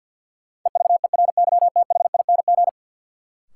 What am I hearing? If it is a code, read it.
Morse code, heard as a keyed tone, about 52 wpm, 700 Hz. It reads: EVERYTHING
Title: Sentences from Top 700 Words - New Word Drill - 50wpm